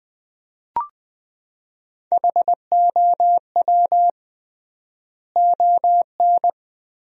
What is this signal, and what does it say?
Morse code, heard as a keyed tone, about 20 wpm, 700 Hz. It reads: HOW ON
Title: 3x Word-Spacing - Sentences from Top 200 Words - 20wpm